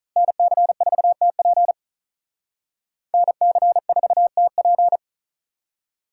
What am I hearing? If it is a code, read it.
Morse code, heard as a keyed tone, about 43 wpm, 700 Hz. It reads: NC4TP NC4TP
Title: Speed Racing - USA 2x2 Call Signs 35wpm